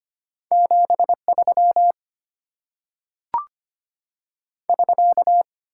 Morse code, heard as a keyed tone, about 25 wpm, 690 Hz. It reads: 73 <SK>
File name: Top 90 QSO Elements - 8x Character Spacing - 25wpm